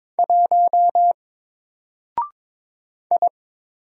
Morse code, heard as a keyed tone, about 22 wpm, 700 Hz. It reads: 1 I
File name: Single-Letter-Number-22wpm